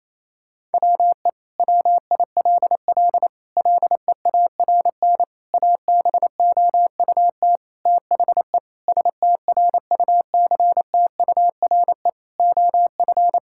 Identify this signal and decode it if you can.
Morse code, heard as a keyed tone, about 28 wpm, 700 Hz. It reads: WE WILL LEARN ABOUT THE STRUCTURE OF
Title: Sentences from Top 1000 Words - Review Part 6 - 28wpm